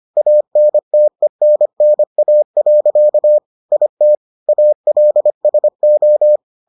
Morse code, heard as a keyed tone, about 25 wpm, 600 Hz. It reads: ANTENNA. IT ALSO